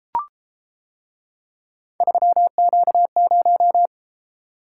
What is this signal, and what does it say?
Morse code, heard as a keyed tone, about 30 wpm, 700 Hz. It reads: 3Q0